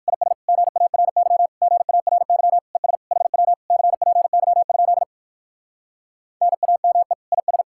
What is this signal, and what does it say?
Morse code, heard as a keyed tone, about 58 wpm, 700 Hz. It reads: IS GARY GARY ES HW CPY? NAME IS